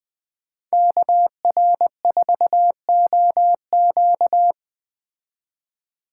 Morse code, heard as a keyed tone, about 21 wpm, 700 Hz. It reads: KR4OQ